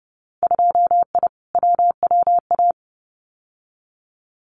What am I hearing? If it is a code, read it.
Morse code, heard as a keyed tone, about 30 wpm, 700 Hz. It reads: 2I WWA
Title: Winter Field Day Exchanges Encore 30wpm